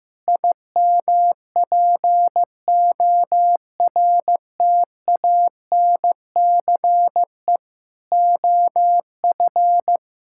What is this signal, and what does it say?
Morse code, heard as a keyed tone, about 15 wpm, 700 Hz. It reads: IMPORTANCE OF
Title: Sentences from Top 1600 Words 15wpm